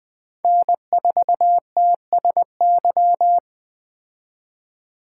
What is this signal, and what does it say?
Morse code, heard as a keyed tone, about 20 wpm, 700 Hz. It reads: N4TSY
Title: Speed Racing - USA 1x3 Call Signs 17wpm